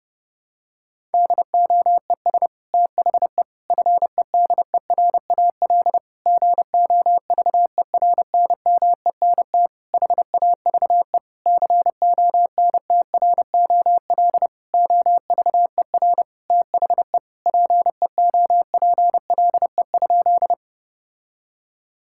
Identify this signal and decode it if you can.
Morse code, heard as a keyed tone, about 30 wpm, 700 Hz. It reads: DOES THE FEDERAL GOVERNMENT HAVE CONTROL OVER THE PEOPLE?
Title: Sentences from Top 1000 Words - Review Part 5 - 30wpm